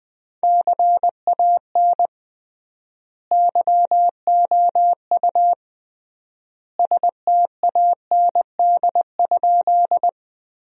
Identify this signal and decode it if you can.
Morse code, heard as a keyed tone, about 20 wpm, 700 Hz. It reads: CAN YOU STAND?
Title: 3x Word-Spacing - Sentences from Top 300 Words - 20wpm